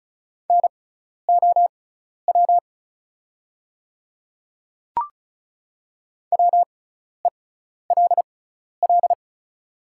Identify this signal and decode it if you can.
Morse code, heard as a keyed tone, about 36 wpm, spaced as for 12 wpm, 700 Hz. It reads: NOW WELL
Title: 6x Character Spacing - Top 200 Words - 35wpm